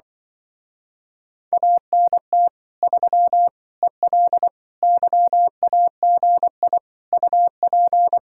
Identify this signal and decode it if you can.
Morse code, heard as a keyed tone, about 24 wpm, 700 Hz. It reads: ANT 3 EL YAGI UP
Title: Speed-Racing Common QSO Phrases Encore - 20wpm